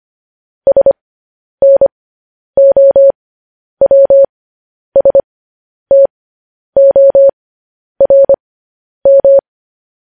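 Morse code, heard as a keyed tone, about 25 wpm, spaced as for 10 wpm, 550 Hz. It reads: SNOWSTORM
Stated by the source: Morse code news headlines broadcast at 10 wpm.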